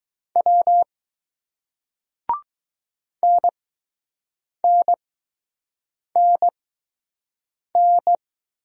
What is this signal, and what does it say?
Morse code, heard as a keyed tone, about 20 wpm, spaced as for 6 wpm, 700 Hz. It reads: W NNNN